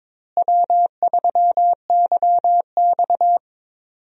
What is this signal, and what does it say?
Morse code, heard as a keyed tone, about 22 wpm, 700 Hz. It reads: W3YX
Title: USA Call Signs - 4 Characters - Repeated 3x - 22wpm